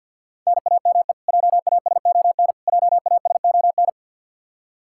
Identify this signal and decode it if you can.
Morse code, heard as a keyed tone, about 49 wpm, 700 Hz. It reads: NAME JASON JASON
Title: Practice QSO - Exchange 1 - Beginning - 50wpm